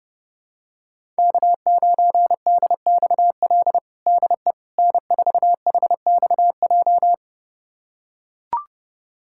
Morse code, heard as a keyed tone, about 32 wpm, 700 Hz. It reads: K9DXL DE N4HXJ
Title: USA Callsign DE Callsign - 5 Characters - 30wpm